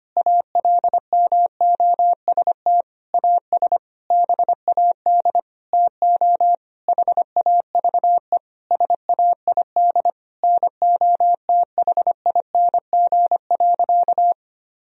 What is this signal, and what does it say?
Morse code, heard as a keyed tone, about 25 wpm, 700 Hz. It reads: ALMOST AS BAD TO HAVE SAID NOTHING.
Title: Sentences from Top 1000 Words - Review Part 4 - 25wpm